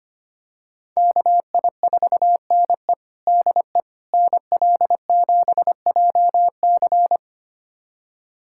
Morse code, heard as a keyed tone, about 25 wpm, 700 Hz. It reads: KI4NE DE NL7JC
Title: USA Callsign DE Callsign 25wpm